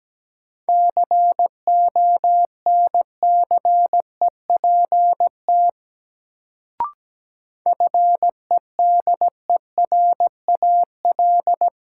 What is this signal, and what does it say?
Morse code, heard as a keyed tone, about 17 wpm, 700 Hz. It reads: CONCEPT FEDERAL
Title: Sentences from Top 1100 Words - New Word Drill Alt - 17wpm